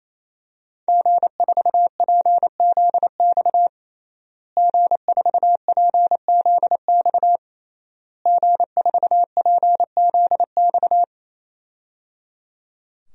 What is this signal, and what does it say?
Morse code, heard as a keyed tone, about 28 wpm, 700 Hz. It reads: G4PZX G4PZX G4PZX
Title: DXCC-100 Call Signs - 5 Characters - Repeated 3x - 28wpm